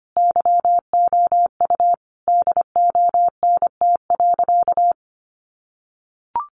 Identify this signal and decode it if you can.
Morse code, heard as a keyed tone, about 25 wpm, 700 Hz. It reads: YOU DONT.
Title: Sentences from Top 600 Words 25wpm